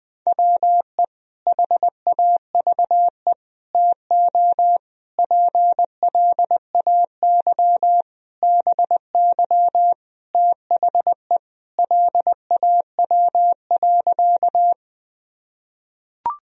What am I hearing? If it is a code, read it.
Morse code, heard as a keyed tone, about 20 wpm, 700 Hz. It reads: WE HAVE TO PLAY BY THE LAW.
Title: Sentences from Top 1000 Words - Review Part 3 - 20wpm